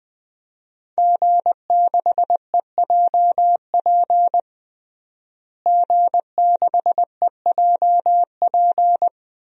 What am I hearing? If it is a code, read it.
Morse code, heard as a keyed tone, about 20 wpm, 700 Hz. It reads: G6EJP G6EJP
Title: DXCC-100 Call Signs - 5 Characters - Repeated 2x - 20wpm